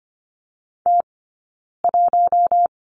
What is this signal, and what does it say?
Morse code, heard as a keyed tone, about 25 wpm, 700 Hz. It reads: T 1
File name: Lesson 15d - Number 5 - Two Characters 25wpm